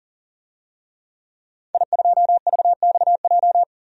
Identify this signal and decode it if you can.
Morse code, heard as a keyed tone, about 40 wpm, 700 Hz. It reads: I2VXJ
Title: DXCC-100 Call Signs - 5 Characters - 40wpm